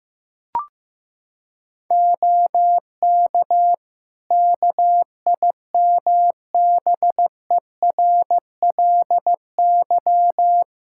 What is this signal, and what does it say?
Morse code, heard as a keyed tone, about 15 wpm, 700 Hz. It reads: OK KIMBERLY
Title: Practice QSO - Exchange 4 - Ending - 15wpm